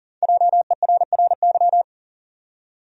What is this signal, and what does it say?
Morse code, heard as a keyed tone, about 40 wpm, 700 Hz. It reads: JERRY
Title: USA Common Names - 40wpm